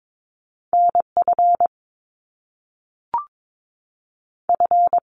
Morse code, heard as a keyed tone, about 22 wpm, spaced as for 3 wpm, 700 Hz. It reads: N FF
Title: Top Two Letter Combinations in Words 22wpm